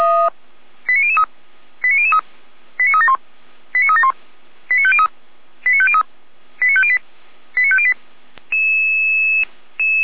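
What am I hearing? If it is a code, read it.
Five-tone selective call (ZVEI), heard as quick runs of five tones, tones of about 80 ms. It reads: 89003 89003 89371 89371 89503 89503 89508 89508
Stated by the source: ZVEI five-tone selective call received by SDR, served by Feuerwehr Orlamünde.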